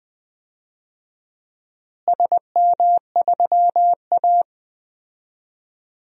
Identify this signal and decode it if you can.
Morse code, heard as a keyed tone, about 21 wpm, 700 Hz. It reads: SM3A
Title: DXCC-100 Call Signs - 4 Characters - Repeated 3x - 20wpm